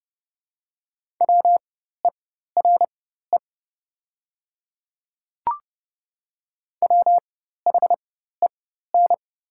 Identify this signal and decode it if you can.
Morse code, heard as a keyed tone, about 29 wpm, spaced as for 14 wpm, 700 Hz. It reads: WERE WHEN
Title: 4x Character Spacing - Top 500 Words - 30wpm